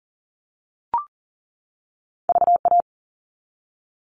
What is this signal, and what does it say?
Morse code, heard as a keyed tone, about 40 wpm, 700 Hz. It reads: VA
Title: All Cut Numbers - 2-Digit 40wpm